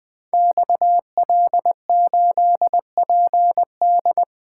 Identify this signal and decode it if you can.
Morse code, heard as a keyed tone, about 20 wpm, 700 Hz. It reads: XL8PD
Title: Code Groups - 5 Letter:Numbers - 20wpm